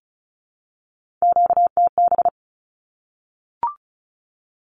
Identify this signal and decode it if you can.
Morse code, heard as a keyed tone, about 34 wpm, 700 Hz. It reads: QTB
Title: Q-Codes Repeated 3x 35wpm